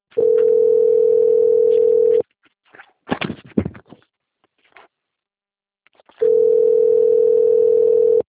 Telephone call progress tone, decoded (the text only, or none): ringback tone